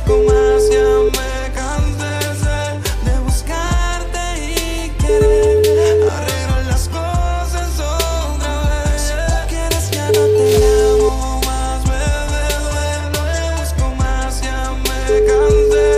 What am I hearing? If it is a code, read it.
Ringback tone (Europe, 425 Hz), with noise.